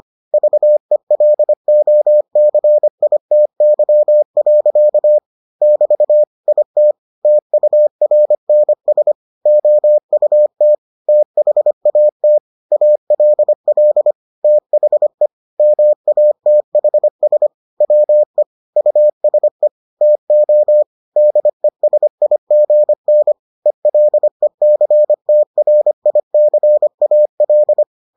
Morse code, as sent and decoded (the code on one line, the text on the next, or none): ...- . .-.. --- -.-. .. - -.-- .-.-.- / -...- / .. - / - ..- .-. -. ... / --- ..- - / - .... .- - / .- .-.. .-.. / - .... / -- .- - .... ... / .-- . / ..- ... / - --- / -.. ... .. --. -. / . .-.. -.-. - .-. .. -.-. .- .-..
VELOCITY. <BT> IT TURNS OUT THAT ALL THE MATHS WE USE TO DESIGN ELECTRICAL